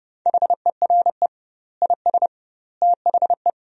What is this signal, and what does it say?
Morse code, heard as a keyed tone, about 29 wpm, 700 Hz. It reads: HERE IS THE
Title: 2x Word-Spacing - Sentences from Top 300 Words - 30wpm